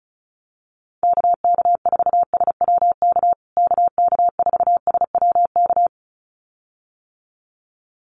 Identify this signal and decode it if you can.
Morse code, heard as a keyed tone, about 34 wpm, 700 Hz. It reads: KK4SWK KK4SWK
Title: USA Call Signs Repeated 2x 35wpm